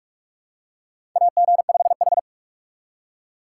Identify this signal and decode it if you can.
Morse code, heard as a keyed tone, about 44 wpm, 700 Hz. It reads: AG5H
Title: USA Call Signs - 4 Characters - 6x Character Spacing - 45wpm